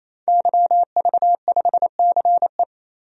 Morse code, heard as a keyed tone, about 28 wpm, 700 Hz. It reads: YV5CE
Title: DXCC-100 Call Signs - 5 Characters - 28wpm